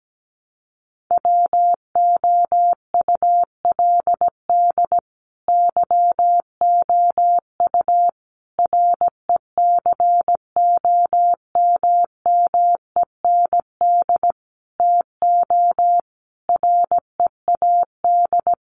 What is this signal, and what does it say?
Morse code, heard as a keyed tone, about 17 wpm, 700 Hz. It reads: WOULD YOU RECOMMEND TO READ
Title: Sentences from Top 1200 Words 17wpm